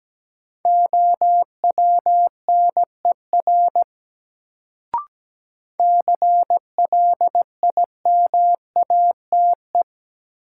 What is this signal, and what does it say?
Morse code, heard as a keyed tone, about 17 wpm, 700 Hz. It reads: OWNER CLIMATE
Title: Sentences from Top 1600 Words - New Word Drill Alt - 17wpm